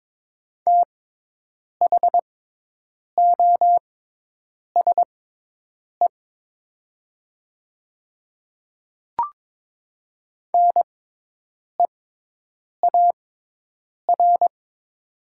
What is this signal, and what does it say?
Morse code, heard as a keyed tone, about 22 wpm, spaced as for 8 wpm, 700 Hz. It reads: THOSE NEAR
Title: 6x Character Spacing - Top 300 Words - 22wpm